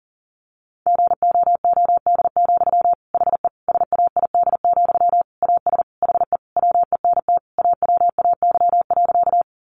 Morse code, heard as a keyed tone, about 40 wpm, 700 Hz. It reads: GOOD, HE SAID, AS HE WENT AWAY.
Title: Sentences from Top 600 Words 40wpm